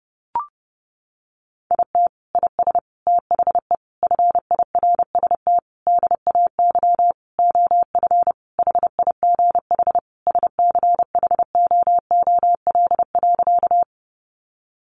Morse code, heard as a keyed tone, about 30 wpm, 700 Hz. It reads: IT IS THE FIRST DAY OF HIGH SCHOOL.
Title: Additional Easy Phrases 30wpm